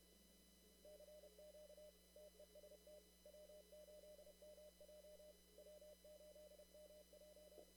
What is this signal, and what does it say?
Morse code, heard as a keyed tone, about 30 wpm, 600 Hz. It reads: CQ TEST W8MJ W8MJ
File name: CW / kt5j wpx cw-op-k5pi